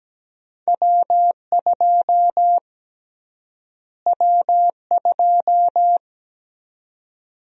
Morse code, heard as a keyed tone, about 17 wpm, 700 Hz. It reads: W2 W2
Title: USA Call Sign Prefixes - Repeated 2x - 17wpm